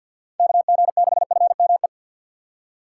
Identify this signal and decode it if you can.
Morse code, heard as a keyed tone, about 63 wpm, 700 Hz. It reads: KG6FME